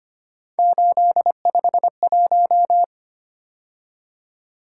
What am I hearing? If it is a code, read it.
Morse code, heard as a keyed tone, about 31 wpm, 700 Hz. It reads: 851
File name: Speed-Racing - Three Letter-Number Combinations - 25wpm